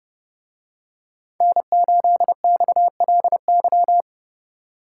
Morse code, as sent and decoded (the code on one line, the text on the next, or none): -. ---.. -..- .-.. -.--
N8XLY